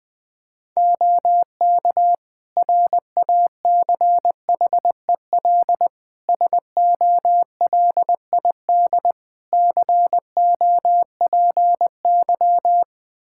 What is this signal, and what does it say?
Morse code, heard as a keyed tone, about 20 wpm, 700 Hz. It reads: OK RACHEL SOLID COPY
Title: K1USN SST QSO Practice 20wpm